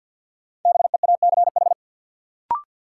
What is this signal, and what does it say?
Morse code, heard as a keyed tone, about 46 wpm, 700 Hz. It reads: BEACH
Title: Sentences from Top 1500 Words - New Word Drill Alt - 50wpm